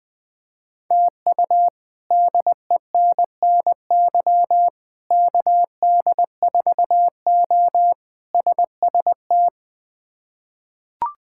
Morse code, heard as a keyed tone, about 20 wpm, 700 Hz. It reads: TU DENNY KD4O SST
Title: K1USN SST QSO Practice - 20wpm